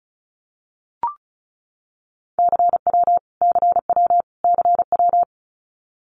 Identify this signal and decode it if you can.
Morse code, heard as a keyed tone, about 34 wpm, 700 Hz. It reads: CW CW CW
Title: Abbreviations Repeated 3x - 35wpm